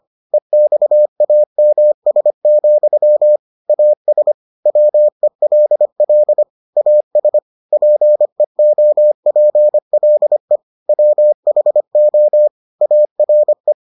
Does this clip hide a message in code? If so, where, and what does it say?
Yes: Morse code from 0.3 to 13.7 s, heard as a keyed tone, about 25 wpm, 600 Hz. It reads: EXAMS, AS WELL AS PEOPLE WHO ARE